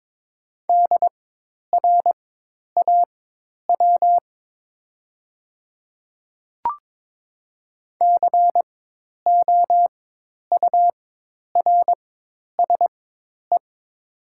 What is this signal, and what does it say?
Morse code, heard as a keyed tone, about 22 wpm, spaced as for 10 wpm, 700 Hz. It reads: DRAW COURSE
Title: 4x Character Spacing - Top 500 Words - 22wpm